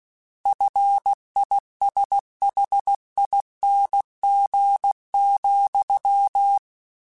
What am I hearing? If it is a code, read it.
Morse code, heard as a keyed tone, about 16 wpm, 800 Hz. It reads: FISHING,
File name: Combinations / Combinations-mp3s / Combinations@15